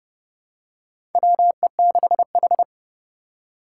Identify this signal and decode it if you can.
Morse code, heard as a keyed tone, about 31 wpm, 700 Hz. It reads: WE6H
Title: USA Call Signs - 4 Characters - 6x Character Spacing - 30wpm